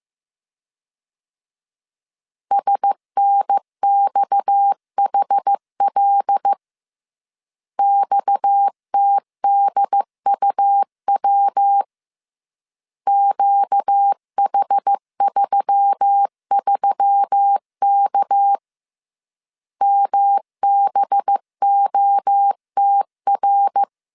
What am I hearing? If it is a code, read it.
Morse code, heard as a keyed tone, about 15 wpm, 800 Hz. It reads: SNXHL XTDUW QH33K MBOTR
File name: CW / DET / 1 6 1500C S C 14wpm 2951